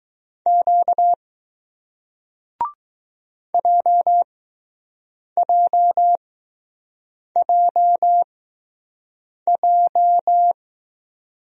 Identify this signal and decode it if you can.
Morse code, heard as a keyed tone, about 20 wpm, spaced as for 6 wpm, 700 Hz. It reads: Q JJJJ